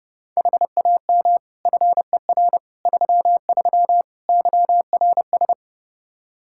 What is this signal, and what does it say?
Morse code, heard as a keyed tone, about 32 wpm, 700 Hz. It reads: HAM FER 33 YRS
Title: Practice QSO - Exchange 3 - More Info - 30wpm